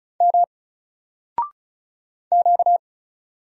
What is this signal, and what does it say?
Morse code, heard as a keyed tone, about 36 wpm, 700 Hz. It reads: M Q
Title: Single Letter-Number 35wpm